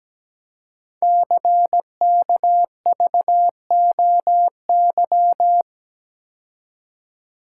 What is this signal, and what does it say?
Morse code, heard as a keyed tone, about 17 wpm, 700 Hz. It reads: CKVOY
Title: Code Groups - 5 Letters - 17wpm